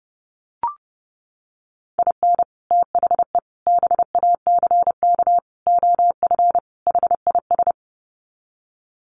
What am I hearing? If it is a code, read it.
Morse code, heard as a keyed tone, about 30 wpm, 700 Hz. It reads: IN THE BACK OF HIS
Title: Sets of 5 Words Encore2 30wpm